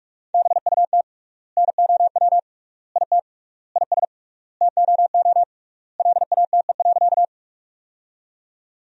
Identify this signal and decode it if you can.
Morse code, heard as a keyed tone, about 46 wpm, 700 Hz. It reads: BUT NOW IT IS TOO LATE.
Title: 3x Word-Spacing - Sentences from Top 300 Words - 45wpm